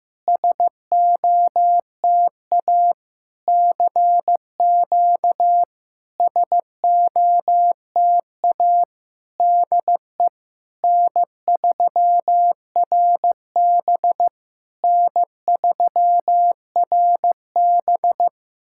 Morse code, heard as a keyed tone, about 15 wpm, 700 Hz. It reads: SOTA CQ SOTA DE N3RB N3RB
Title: SOTA QSO Practice 15wpm